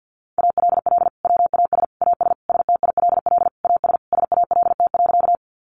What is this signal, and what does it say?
Morse code, heard as a keyed tone, about 50 wpm, 700 Hz. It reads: ALL WAS AS STILL AS SALT.
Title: Sentences from Top 800 Words 50wpm